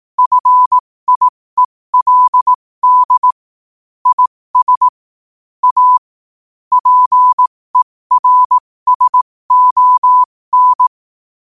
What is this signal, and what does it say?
Morse code, heard as a keyed tone, about 18 wpm, spaced as for 15 wpm, 1000 Hz. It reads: FIELD IS A PERSON
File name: qotdfiles / 15wpm